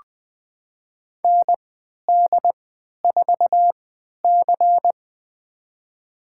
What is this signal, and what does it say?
Morse code, heard as a keyed tone, about 20 wpm, spaced as for 11 wpm, 700 Hz. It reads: ND4C
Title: USA Call Signs - 4 Characters - 3x Character Spacing - 20wpm